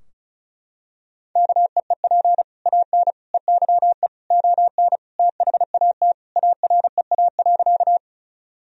Morse code, heard as a keyed tone, about 34 wpm, 700 Hz. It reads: KEEP AN EYE ON THAT AREA.
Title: Sentences from Top 1000 Words - Review Part 2 - 35wpm